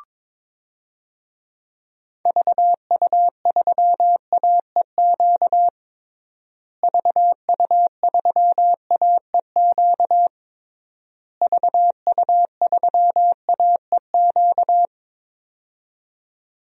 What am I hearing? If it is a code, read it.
Morse code, heard as a keyed tone, about 22 wpm, 700 Hz. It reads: VU3AEQ VU3AEQ VU3AEQ
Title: DXCC-100 Call Signs - 6 Characters - Repeated 3x - 22wpm